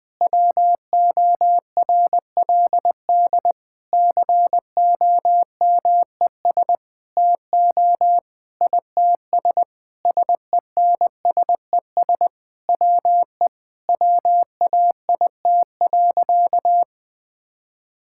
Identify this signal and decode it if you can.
Morse code, heard as a keyed tone, about 21 wpm, 700 Hz. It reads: WORLD COMES TO ITS SENSES WE WAIT.
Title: Sentences with 9-Word Phrases - 20wpm